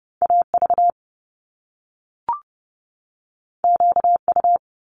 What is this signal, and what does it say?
Morse code, heard as a keyed tone, about 30 wpm, 700 Hz. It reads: AV QU
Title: Top Two Letter Combinations in Words 30wpm